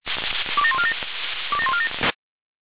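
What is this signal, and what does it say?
Five-tone selective call (ZVEI), heard as quick runs of five tones, tones of about 70 ms. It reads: 27147 27147
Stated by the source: ZVEI five-tone call 27161 received on SDR.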